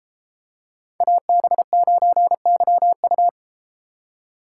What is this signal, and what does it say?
Morse code, heard as a keyed tone, about 32 wpm, 700 Hz. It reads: AB9YU